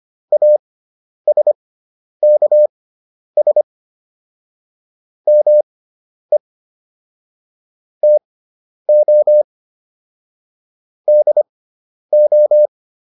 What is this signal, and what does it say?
Morse code, heard as a keyed tone, about 25 wpm, spaced as for 10 wpm, 600 Hz. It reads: ASKS ME TO DO